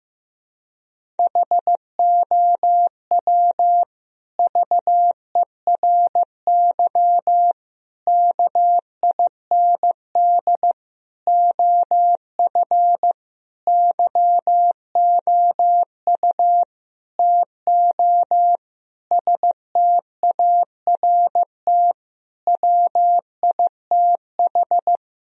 Morse code, heard as a keyed tone, about 16 wpm, 700 Hz. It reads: HOW VERY KIND OF YOU TO START WITH
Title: Sentences from Top 1000 Words - Review Part 1 - 15wpm